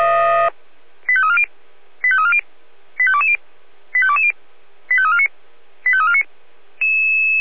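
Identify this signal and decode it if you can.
Five-tone selective call (ZVEI), heard as quick runs of five tones, tones of about 70 ms. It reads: 85340 85340 85220 85220 85349 85349
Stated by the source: ZVEI five-tone call 85351 received on SDR.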